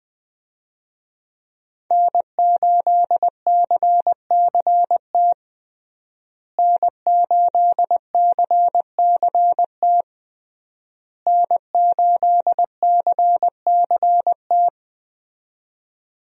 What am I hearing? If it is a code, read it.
Morse code, heard as a keyed tone, about 20 wpm, 700 Hz. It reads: N8CCT N8CCT N8CCT
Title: USA Call Signs - 5 Characters - Repeated 3x - 20wpm